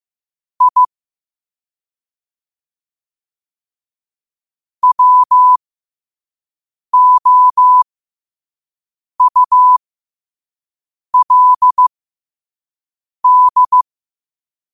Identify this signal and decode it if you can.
Morse code, heard as a keyed tone, about 15 wpm, spaced as for 5 wpm, 1000 Hz. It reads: I WOULD
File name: qotdfiles / 5wpm